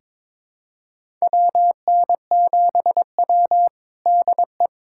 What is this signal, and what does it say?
Morse code, heard as a keyed tone, about 22 wpm, 700 Hz. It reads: WN7W DE
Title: USA Callsign DE Callsign - 4 Characters - 22wpm